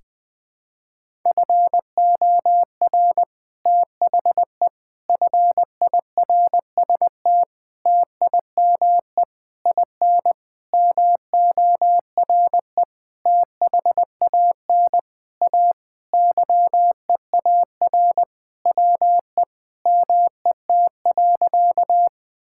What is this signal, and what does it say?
Morse code, heard as a keyed tone, about 20 wpm, 700 Hz. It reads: FOR THE FIRST TIME IN MORE THAN A YEAR WE MET.
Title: Sentences with 7-Word Phrases - Spoken Beginning - 20wpm